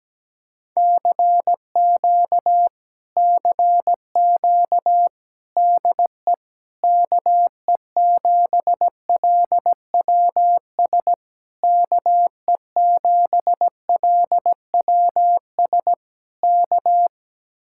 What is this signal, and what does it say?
Morse code, heard as a keyed tone, about 17 wpm, 700 Hz. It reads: CQ CQ DE KE7LWS KE7LWS K
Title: Speed-Racing Common QSO Phrases - 17wpm